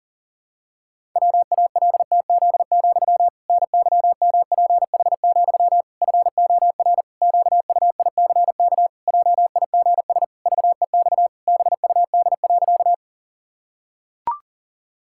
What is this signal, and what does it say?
Morse code, heard as a keyed tone, about 40 wpm, 700 Hz. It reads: WALTZ, NYMPH, FOR QUICK JIGS VEX BUD.
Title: Pangram Challenge v2 - 40wpm